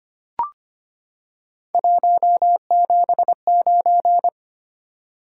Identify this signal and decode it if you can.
Morse code, heard as a keyed tone, about 25 wpm, 700 Hz. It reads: 179
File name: Three Letter-Number Combinations - 25wpm